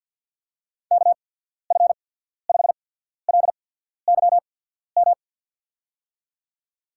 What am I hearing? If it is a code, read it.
Morse code, heard as a keyed tone, about 48 wpm, spaced as for 14 wpm, 690 Hz. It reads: KF5LYM
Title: USA Call Signs - 6 Characters - 8x Character Spacing - 50wpm